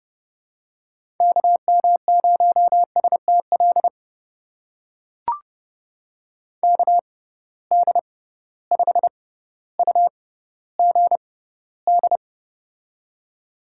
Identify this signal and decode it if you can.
Morse code, heard as a keyed tone, about 30 wpm, spaced as for 15 wpm, 700 Hz. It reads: KM0STL KD5UGD
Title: USA Call Signs - 6 Characters - 6x Character Spacing - 30wpm